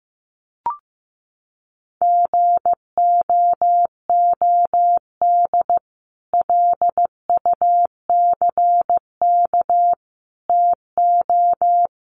Morse code, heard as a keyed tone, about 15 wpm, 700 Hz. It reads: GOOD LUCK TO
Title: Sentences from Top 1500 Words 15wpm